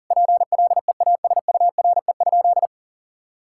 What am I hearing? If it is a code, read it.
Morse code, heard as a keyed tone, about 40 wpm, 700 Hz. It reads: PLEASURE?
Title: Sentences from Top 1600 Words 40wpm